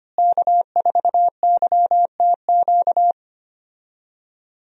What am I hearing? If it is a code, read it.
Morse code, heard as a keyed tone, about 25 wpm, 700 Hz. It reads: K4YTQ
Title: USA Call Signs - 5 Characters - Repeated 3x - 25wpm